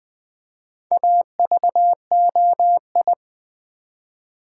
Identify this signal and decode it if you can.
Morse code, heard as a keyed tone, about 20 wpm, 700 Hz. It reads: AVOI
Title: Build A Word - 1 Letter At A Time - 5-Letter Words - 20wpm